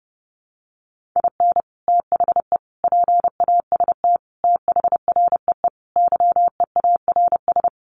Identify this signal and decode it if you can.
Morse code, heard as a keyed tone, about 30 wpm, 700 Hz. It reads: IN THE PAST THREE YEARS
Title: Sets of 5 Words Encore2 30wpm